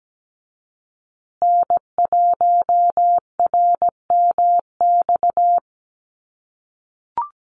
Morse code, heard as a keyed tone, about 17 wpm, 700 Hz. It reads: N1RMX
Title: USA Call Signs Repeated 2x 17wpm